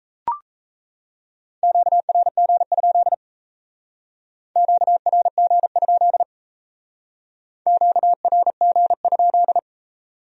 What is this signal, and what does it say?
Morse code, heard as a keyed tone, about 35 wpm, 700 Hz. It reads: QRG? QRG? QRG?
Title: Speed-Racing Q-Codes - 28wpm